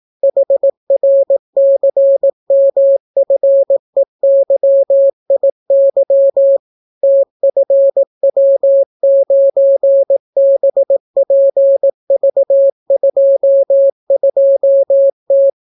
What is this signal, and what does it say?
Morse code, heard as a keyed tone, about 18 wpm, 550 Hz. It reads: HRCMFEYIY TFW9BPV22T